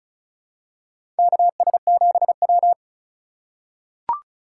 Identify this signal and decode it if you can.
Morse code, heard as a keyed tone, about 36 wpm, 700 Hz. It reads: KS7W
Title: USA Call Signs - 3x Character Spacing - 35wpm